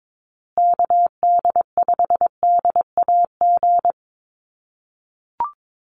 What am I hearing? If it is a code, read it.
Morse code, heard as a keyed tone, about 22 wpm, 700 Hz. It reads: KD5DAG
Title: USA Call Signs Repeated 2x 22wpm